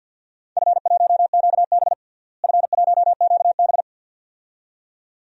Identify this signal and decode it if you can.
Morse code, heard as a keyed tone, about 51 wpm, 700 Hz. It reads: F1QB F1QB
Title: DXCC-100 Call Signs - 4 Characters - Repeated 2x - 50wpm